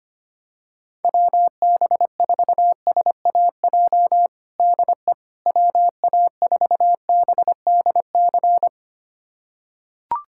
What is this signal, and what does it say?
Morse code, heard as a keyed tone, about 25 wpm, 700 Hz. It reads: WB4SAJ DE WA4BDC
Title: USA Callsign DE Callsign - 6 Characters - 25wpm